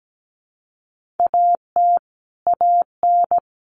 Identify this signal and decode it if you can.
Morse code, heard as a keyed tone, about 17 wpm, 700 Hz. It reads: AT AN
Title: Sentences from Top 1500 Words 17wpm